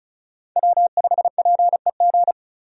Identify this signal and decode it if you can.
Morse code, heard as a keyed tone, about 36 wpm, 700 Hz. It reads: W5PEG
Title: USA Call Signs - 5 Characters - 35wpm